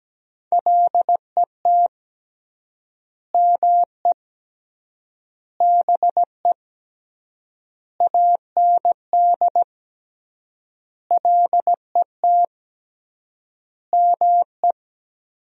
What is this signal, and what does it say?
Morse code, heard as a keyed tone, about 17 wpm, 700 Hz. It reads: LET ME BE AND LET ME
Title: 3x Word-Spacing - Sentences from Top 300 Words - 17wpm